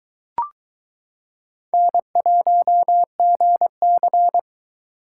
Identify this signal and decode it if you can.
Morse code, heard as a keyed tone, about 22 wpm, 700 Hz. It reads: N1GC